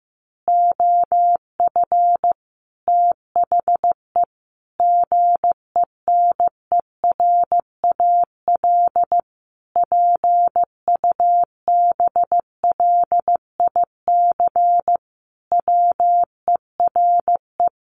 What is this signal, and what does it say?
Morse code, heard as a keyed tone, about 15 wpm, 700 Hz. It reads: OF THE GENERAL PUBLIC WERE
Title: Sentences from Top 800 Words 15wpm